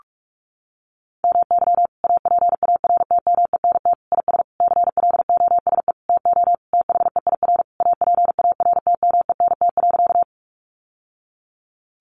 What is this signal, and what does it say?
Morse code, heard as a keyed tone, about 44 wpm, 700 Hz. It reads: MY APARTMENT IS CLOSE TO THEIR APARTMENT.
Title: Sentences from Top 900 Words 45wpm